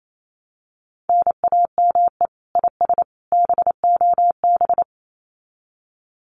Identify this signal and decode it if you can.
Morse code, heard as a keyed tone, about 28 wpm, 700 Hz. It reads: NAME IS BOB
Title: Common QSO Phrases 28wpm